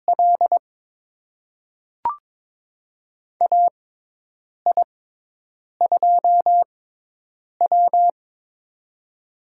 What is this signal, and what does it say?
Morse code, heard as a keyed tone, about 21 wpm, spaced as for 8 wpm, 700 Hz. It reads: L AI2W
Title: USA Call Signs - 4 Characters - 6x Character Spacing - 22wpm